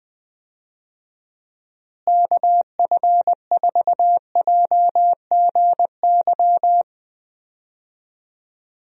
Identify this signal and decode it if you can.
Morse code, heard as a keyed tone, about 20 wpm, 700 Hz. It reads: KF4JGY
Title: USA Call Signs - 6 Characters - Repeated 2x - 20wpm